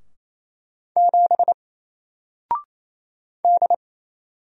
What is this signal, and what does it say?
Morse code, heard as a keyed tone, about 27 wpm, 700 Hz. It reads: 7 D
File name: Single-Letter-Number-28wpm